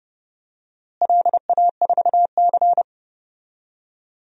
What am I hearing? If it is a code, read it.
Morse code, heard as a keyed tone, about 30 wpm, 700 Hz. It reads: LA4C
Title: DXCC-100 Call Signs - 4 Characters - Repeated 2x - 30wpm